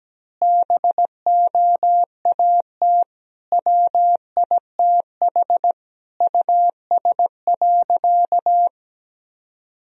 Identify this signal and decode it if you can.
Morse code, heard as a keyed tone, about 17 wpm, 700 Hz. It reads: BOAT WITH US.
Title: Sentences from Top 1000 Words - Review Part 3 - 17wpm